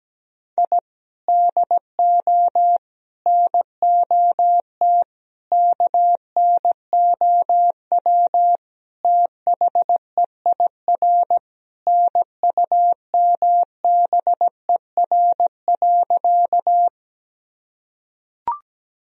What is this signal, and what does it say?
Morse code, heard as a keyed tone, about 17 wpm, 700 Hz. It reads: I DO NOT KNOW THEIR NUMBER.
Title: Sentences from Top 1000 Words - Review Part 1 - 17wpm